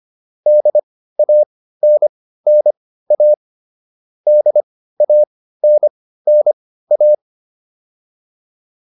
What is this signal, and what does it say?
Morse code, heard as a keyed tone, about 25 wpm, spaced as for 15 wpm, 600 Hz. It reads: DANNA DANNA